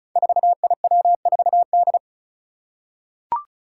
Morse code, heard as a keyed tone, about 36 wpm, 700 Hz. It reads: 4IW4D
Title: Five Challenging Letter-Number Combinations 35wpm